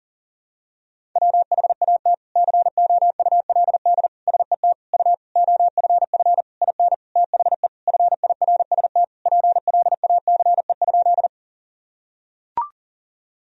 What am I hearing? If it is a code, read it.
Morse code, heard as a keyed tone, about 40 wpm, 700 Hz. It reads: WHAT COULD SET U OFF IN THE FIRST PLACE?
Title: Sentences from Top 200 Words and Abbreviations 40wpm